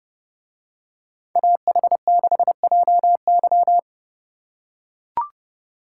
Morse code, heard as a keyed tone, about 28 wpm, 700 Hz. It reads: AH6JY